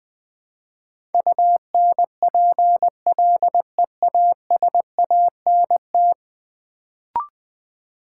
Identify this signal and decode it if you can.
Morse code, heard as a keyed tone, about 20 wpm, 700 Hz. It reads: UNPLEASANT
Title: Build A Word - 5 Letters At A Time - 10-Letter Words - 20wpm